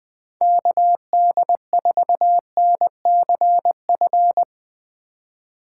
Morse code, heard as a keyed tone, about 20 wpm, 700 Hz. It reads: KD4NCF
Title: Speed Racing - USA 2x3 Call Signs 17wpm